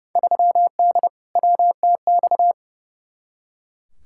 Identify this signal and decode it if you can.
Morse code, heard as a keyed tone, about 30 wpm, 700 Hz. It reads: 3D WTX
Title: Field Day Exchanges Encore - 30wpm